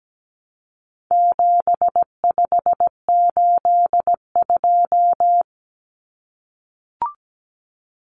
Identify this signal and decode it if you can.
Morse code, heard as a keyed tone, about 17 wpm, 700 Hz. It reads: 7582
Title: Numbers - 4-Digit 17wpm